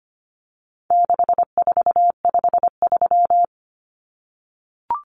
Morse code, heard as a keyed tone, about 25 wpm, 700 Hz. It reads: 6453
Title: Numbers - 4-Digit 25wpm